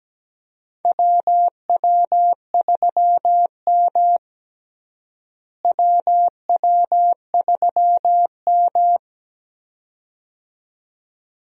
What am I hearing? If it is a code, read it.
Morse code, heard as a keyed tone, about 17 wpm, 700 Hz. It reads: WW3M WW3M
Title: USA Call Signs - 4 Characters - Repeated 2x - 17wpm